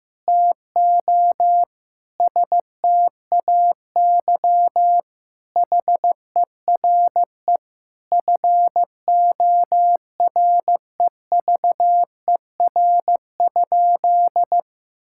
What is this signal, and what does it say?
Morse code, heard as a keyed tone, about 16 wpm, 700 Hz. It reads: TO STAY HERE FOREVER?
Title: Sentences from Top 1600 Words 15wpm